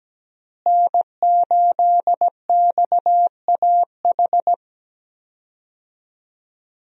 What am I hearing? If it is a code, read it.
Morse code, heard as a keyed tone, about 17 wpm, 700 Hz. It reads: N8XAH
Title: USA Call Signs - 5 Characters - Repeated 3x - 17wpm